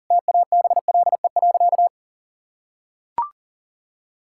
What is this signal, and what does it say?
Morse code, heard as a keyed tone, about 40 wpm, 700 Hz. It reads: TABLE.
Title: Sentences from Top 1600 Words 40wpm